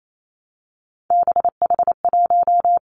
Morse code, heard as a keyed tone, about 28 wpm, 690 Hz. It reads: BH1
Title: Three Challenging Letter-Number Combinations 28wpm